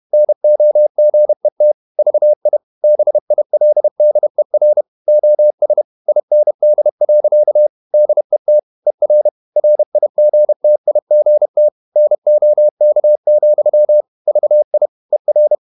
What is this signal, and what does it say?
Morse code, heard as a keyed tone, about 31 wpm, 600 Hz. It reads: NOGET VI BILDER OS IND. DET ER RIGTIGT NOK, VI ER
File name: Ferie QRQ 31wpm